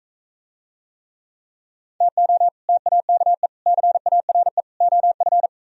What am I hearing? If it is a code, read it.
Morse code, heard as a keyed tone, about 41 wpm, 700 Hz. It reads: TO TAKE CARE OF